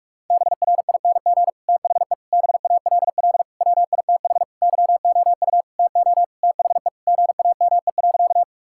Morse code, heard as a keyed tone, about 42 wpm, 700 Hz. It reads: BRING THE BALL WITH YOU TO THE GAME.
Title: Sentences from Top 1000 Words - Review Part 6 - 45wpm